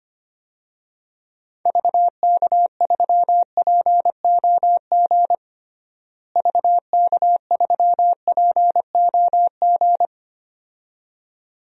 Morse code, heard as a keyed tone, about 25 wpm, 700 Hz. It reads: VK3POG VK3POG
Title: DXCC-100 Call Signs - 6 Characters - Repeated 2x - 25wpm